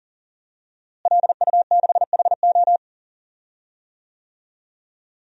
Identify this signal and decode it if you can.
Morse code, heard as a keyed tone, about 40 wpm, 700 Hz. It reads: LU6HO
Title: DXCC-100 Call Signs - 5 Characters - Repeated 3x - 40wpm